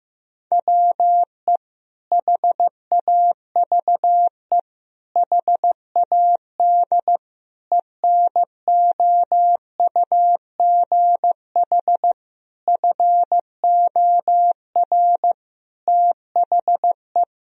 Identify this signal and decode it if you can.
Morse code, heard as a keyed tone, about 15 wpm, 700 Hz. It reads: WE HAVE HAD ENOUGH FOR THE
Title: Sentences from Top 1000 Words - Review Part 3 - 15wpm